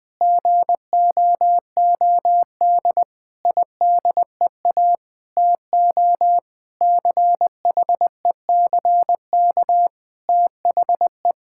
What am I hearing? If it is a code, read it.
Morse code, heard as a keyed tone, about 20 wpm, 700 Hz. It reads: GOOD IDEA TO CHECK THE
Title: Sentences with 6-Word Phrases - 20wpm